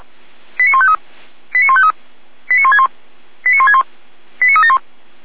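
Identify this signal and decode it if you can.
Five-tone selective call (ZVEI), heard as quick runs of five tones, tones of about 70 ms. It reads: 89153 89153 89161 89161 89371
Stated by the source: ZVEI five-tone selective call received with an SDR, from feuerwehr-orlamuende (fire brigade Orlamünde).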